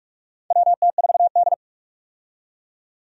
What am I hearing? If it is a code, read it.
Morse code, heard as a keyed tone, about 41 wpm, 700 Hz. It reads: WT4D
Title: USA Call Signs - 4 Characters - 45wpm